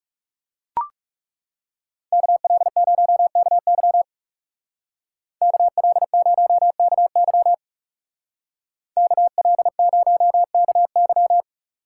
Morse code, heard as a keyed tone, about 38 wpm, 700 Hz. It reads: KL0KY KL0KY KL0KY